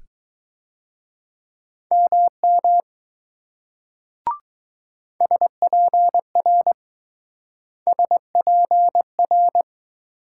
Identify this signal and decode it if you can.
Morse code, heard as a keyed tone, about 19 wpm, 700 Hz. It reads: MM SPR SPR